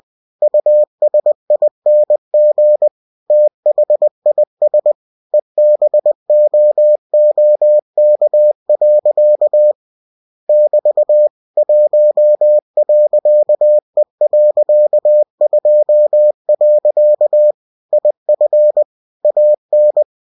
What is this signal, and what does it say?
Morse code, heard as a keyed tone, about 20 wpm, 600 Hz. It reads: USING THIS EBOOK. <BT> 1.E.2. IF AN
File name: winnie-20wpm